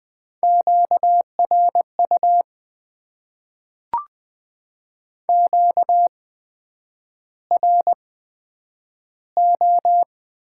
Morse code, heard as a keyed tone, about 20 wpm, spaced as for 9 wpm, 700 Hz. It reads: QRU QRO
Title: Top 90 QSO Elements - 8x Character Spacing - 20wpm